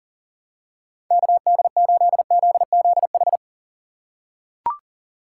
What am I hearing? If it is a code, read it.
Morse code, heard as a keyed tone, about 40 wpm, 700 Hz. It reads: KD8ZZH
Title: USA Call Signs - 6 Characters - 4x Character Spacing - 40wpm